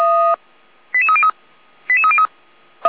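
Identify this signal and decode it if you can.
Five-tone selective call (ZVEI), heard as quick runs of five tones, tones of about 70 ms. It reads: 88303 88303
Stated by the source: ZVEI five-tone selective call, code 88207, received by SDR; Feuerwehr Orlamünde fire brigade.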